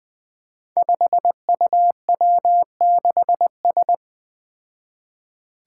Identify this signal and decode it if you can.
Morse code, heard as a keyed tone, about 21 wpm, 700 Hz. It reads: 5UW6S
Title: Five Challenging Letter-Number Combinations 20wpm